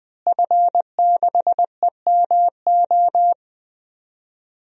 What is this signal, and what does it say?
Morse code, heard as a keyed tone, about 20 wpm, 700 Hz. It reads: F6EMO
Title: DXCC-100 Call Signs - 5 Characters - Repeated 3x - 20wpm